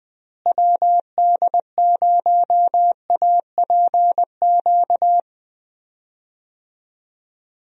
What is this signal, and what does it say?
Morse code, heard as a keyed tone, about 20 wpm, 700 Hz. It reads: WD0APQ
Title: USA Call Signs - 6 Characters - Repeated 3x - 20wpm